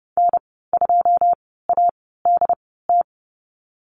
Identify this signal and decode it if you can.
Morse code, heard as a keyed tone, about 30 wpm, spaced as for 17 wpm, 700 Hz. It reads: N2ADT
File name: Lesson 15g - Number 5 - Call Signs 30wpm